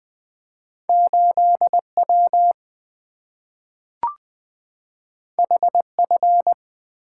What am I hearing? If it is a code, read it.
Morse code, heard as a keyed tone, about 20 wpm, 690 Hz. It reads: 8W HF
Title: Two Letter-Number Combinations - 20wpm